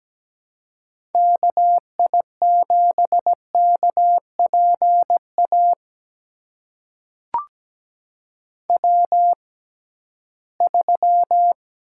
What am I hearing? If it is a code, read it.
Morse code, heard as a keyed tone, about 17 wpm, spaced as for 8 wpm, 700 Hz. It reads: KI7KP AW3
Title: USA Call Signs - 6x Character Spacing - 17wpm